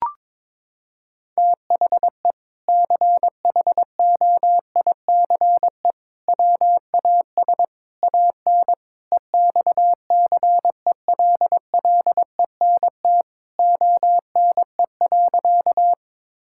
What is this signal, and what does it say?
Morse code, heard as a keyed tone, about 22 wpm, 700 Hz. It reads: THE CHOICE WAS AN EXCELLENT ONE.
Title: Sentences from Top 1000 Words - Review Part 6 - 22wpm